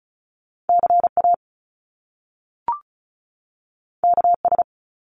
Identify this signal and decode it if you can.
Morse code, heard as a keyed tone, about 35 wpm, 700 Hz. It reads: CA KS
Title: US States 35wpm